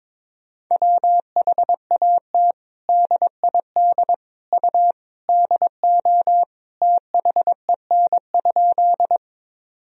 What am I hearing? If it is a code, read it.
Morse code, heard as a keyed tone, about 22 wpm, 700 Hz. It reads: WHAT DID U DO THEN?
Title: Sentences from Top 100 Words and Abbreviations 22wpm